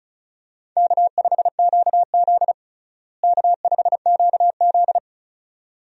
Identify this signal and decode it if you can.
Morse code, heard as a keyed tone, about 36 wpm, 700 Hz. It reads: K5QZ K5QZ
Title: USA Call Signs - 4 Characters - Repeated 2x - 35wpm